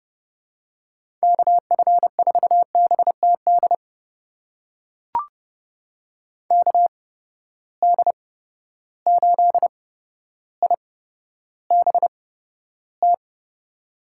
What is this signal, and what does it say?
Morse code, heard as a keyed tone, about 30 wpm, spaced as for 13 wpm, 700 Hz. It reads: KF4BTD KD8IBT
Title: USA Call Signs - 6 Characters - 8x Character Spacing - 30wpm